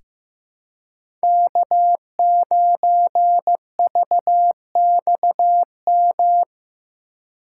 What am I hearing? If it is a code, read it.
Morse code, heard as a keyed tone, about 15 wpm, 700 Hz. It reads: K9VXM
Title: USA Call Signs - 5 Characters - 3x Character Spacing - 15wpm